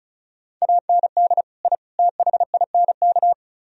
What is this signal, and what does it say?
Morse code, heard as a keyed tone, about 45 wpm, 690 Hz. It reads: AND I THINK